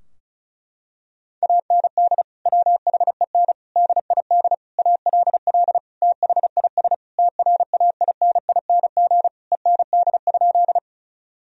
Morse code, heard as a keyed tone, about 36 wpm, 700 Hz. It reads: AND WHEN DID ALL THIS TRAINING END?
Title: Sentences from Top 1000 Words - Review Part 4 - 35wpm